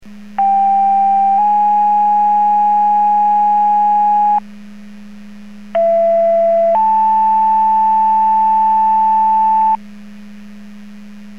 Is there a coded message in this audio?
Yes: two-tone page from 0.4 to 9.8 s, heard as a one-second tone, then a three-second tone, about 790 then 830 Hz, about 670 then 880 Hz.